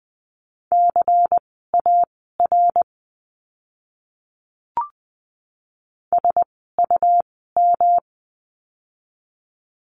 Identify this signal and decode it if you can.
Morse code, heard as a keyed tone, about 20 wpm, spaced as for 15 wpm, 700 Hz. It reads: CAR SUM
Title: 108 Lesson 18f - Letter m - Words 20wpm